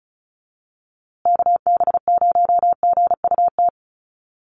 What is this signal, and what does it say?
Morse code, heard as a keyed tone, about 34 wpm, 700 Hz. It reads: KB0GUT
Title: USA Call Signs Repeated 2x 35wpm